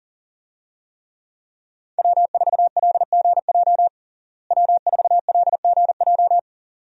Morse code, heard as a keyed tone, about 40 wpm, 700 Hz. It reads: W4LGJ W4LGJ
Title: USA Call Signs - 5 Characters - Repeated 2x - 40wpm